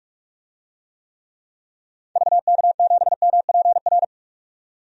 Morse code, heard as a keyed tone, about 46 wpm, 700 Hz. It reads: VK7MPR
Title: DXCC-100 Call Signs - 6 Characters - 45wpm